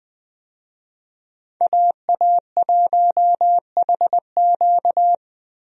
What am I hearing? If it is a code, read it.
Morse code, heard as a keyed tone, about 20 wpm, 700 Hz. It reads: AA1HQ
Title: USA Call Signs - 5 Characters - 20wpm